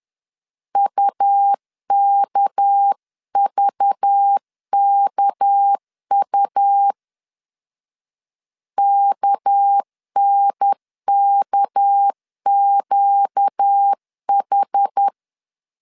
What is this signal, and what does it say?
Morse code, heard as a keyed tone, about 10 wpm, 800 Hz. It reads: UKVKU KNKQH
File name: LEZIONI / 6 PKX0 / 6 PK 10wpm 2436